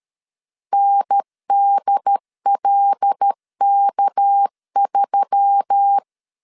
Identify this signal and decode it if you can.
Morse code, heard as a keyed tone, about 13 wpm, 800 Hz. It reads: NDLK3
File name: CW / DET / 1 6 1500C S B 12wpm 3431